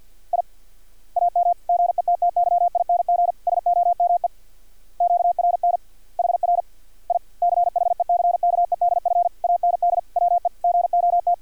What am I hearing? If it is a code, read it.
Morse code, heard as a keyed tone, about 50 wpm, 700 Hz. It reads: I AM GETTYING SOME QRN HR I CHECKEDF AND WE GOT
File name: archived / CRSnet